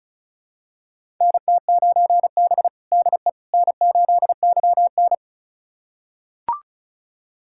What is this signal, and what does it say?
Morse code, heard as a keyed tone, about 34 wpm, 700 Hz. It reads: NT9B DE N8YN
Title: USA Callsign DE Callsign - 4 Characters - 35wpm